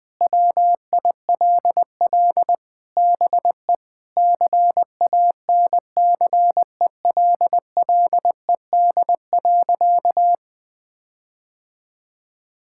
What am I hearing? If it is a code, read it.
Morse code, heard as a keyed tone, about 21 wpm, 700 Hz. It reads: WILL BE CANCELLED.
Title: Sentences with 8-Word Phrases - 20wpm